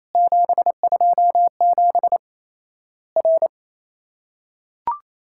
Morse code, heard as a keyed tone, about 28 wpm, 700 Hz. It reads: 727 R
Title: Worked All Europe DX Contest - QTC Exchange - 28wpm